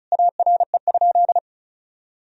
Morse code, heard as a keyed tone, about 36 wpm, 700 Hz. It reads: ARE?
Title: Sentences from Top 1000 Words - Review Part 4 - 35wpm